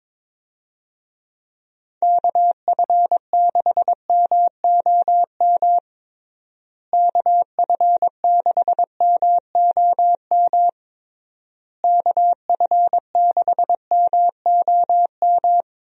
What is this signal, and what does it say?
Morse code, heard as a keyed tone, about 22 wpm, 700 Hz. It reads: KF6MOM KF6MOM KF6MOM
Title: USA Call Signs - 6 Characters - Repeated 3x - 22wpm